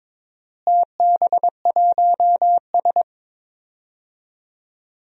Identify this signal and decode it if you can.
Morse code, heard as a keyed tone, about 21 wpm, 700 Hz. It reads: TB1S
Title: DXCC-100 Call Signs - 4 Characters - Repeated 3x - 22wpm